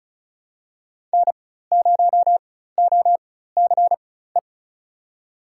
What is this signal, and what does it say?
Morse code, heard as a keyed tone, about 36 wpm, spaced as for 16 wpm, 700 Hz. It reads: N0OCE
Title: USA Call Signs - 5 Characters - 4x Character Spacing - 35wpm